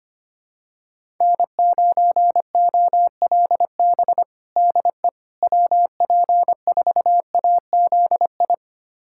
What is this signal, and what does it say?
Morse code, heard as a keyed tone, about 25 wpm, 700 Hz. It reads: N9OLB DE WP4AZI
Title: USA Callsign DE Callsign 25wpm